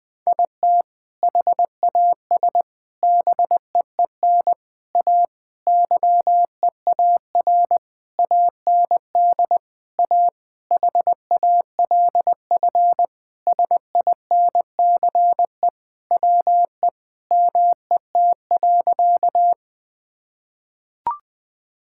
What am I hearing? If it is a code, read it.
Morse code, heard as a keyed tone, about 20 wpm, 700 Hz. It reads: IT HAS BEEN A YEAR AND A HALF SINCE WE MET.
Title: Sentences with 4-Word Phrases - 20wpm